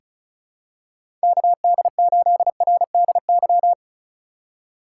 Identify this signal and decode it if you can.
Morse code, heard as a keyed tone, about 34 wpm, 700 Hz. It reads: KD8RDY
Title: USA Call Signs - 6 Characters - 2x Character Spacing - 35wpm